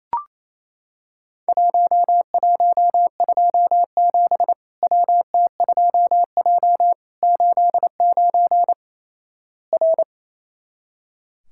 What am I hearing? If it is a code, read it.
Morse code, heard as a keyed tone, about 28 wpm, 700 Hz. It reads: 1127 WT2J 89 R
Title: Worked All Europe DX Contest - QTC Exchange - 28wpm